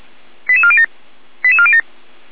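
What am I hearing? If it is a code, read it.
Five-tone selective call (ZVEI), heard as quick runs of five tones, tones of about 70 ms. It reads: 88407 88407
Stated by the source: ZVEI five-tone selective call received on an SDR.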